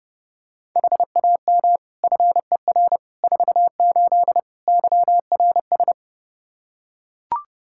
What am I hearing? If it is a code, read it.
Morse code, heard as a keyed tone, about 30 wpm, 700 Hz. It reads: HAM FER 48 YRS
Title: Practice QSO - Exchange 3 - More Info - 30wpm